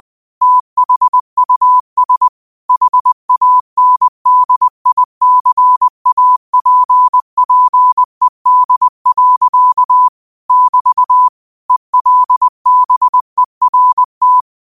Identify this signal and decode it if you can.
Morse code, heard as a keyed tone, about 20 wpm, 1000 Hz. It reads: THUS HANDICAPPED. <BT> ELBERT
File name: qotdfiles / 20wpm